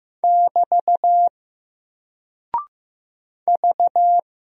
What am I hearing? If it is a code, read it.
Morse code, heard as a keyed tone, about 15 wpm, 700 Hz. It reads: <BT> V
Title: Single Character (Letter:Number:Punctuation:Prosign) Alt 15wpm